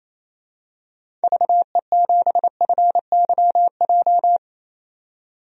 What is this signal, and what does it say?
Morse code, heard as a keyed tone, about 28 wpm, 700 Hz. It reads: VE7FYJ
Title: DXCC-100 Call Signs - 6 Characters - 28wpm